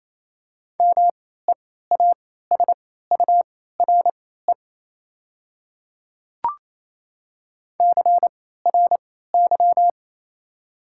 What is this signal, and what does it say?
Morse code, heard as a keyed tone, about 28 wpm, spaced as for 16 wpm, 700 Hz. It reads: MEASURE CRY
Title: 3x Character Spacing - Top 500 Words - 28wpm